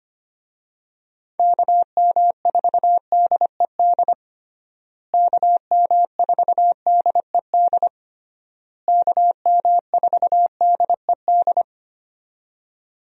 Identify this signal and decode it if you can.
Morse code, heard as a keyed tone, about 25 wpm, 700 Hz. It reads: KM4DED KM4DED KM4DED
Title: USA Call Signs - 6 Characters - Repeated 3x - 25wpm